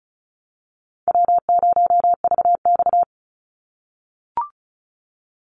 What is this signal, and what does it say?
Morse code, heard as a keyed tone, about 34 wpm, 700 Hz. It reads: W0VX
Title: Call Signs - 35wpm